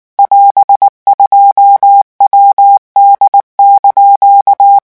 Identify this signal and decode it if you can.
Morse code, heard as a keyed tone, about 19 wpm, 800 Hz. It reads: <AS>2WD)
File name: morse mp3 / revision15-vitesse 18 mots